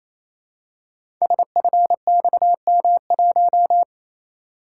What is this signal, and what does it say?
Morse code, heard as a keyed tone, about 27 wpm, 700 Hz. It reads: SFXM1
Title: Code Groups - 5 Letter:Numbers - 28wpm